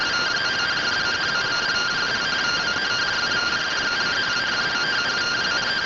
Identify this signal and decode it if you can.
Two-tone teleprinter signal, about 170 Hz shift.